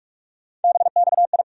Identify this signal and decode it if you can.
Morse code, heard as a keyed tone, about 45 wpm, 700 Hz. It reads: BXI
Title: USA Call Sign Postfixes - 3 Letters 45wpm